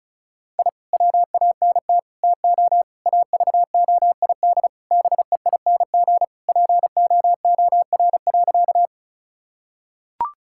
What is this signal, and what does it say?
Morse code, heard as a keyed tone, about 34 wpm, 700 Hz. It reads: I WANT TO AVOID BEING POOR.
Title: Sentences from Top 1000 Words - Review Part 6 - 35wpm